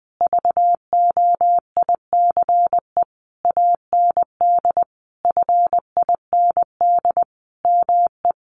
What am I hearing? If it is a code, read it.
Morse code, heard as a keyed tone, about 20 wpm, 700 Hz. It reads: VOICE AND FIND ME
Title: Sentences from Top 500 Words 20wpm